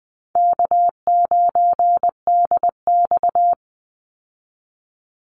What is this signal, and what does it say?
Morse code, heard as a keyed tone, about 20 wpm, 700 Hz. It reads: K9DX
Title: Easy Call Signs Repeated 2x - 20wpm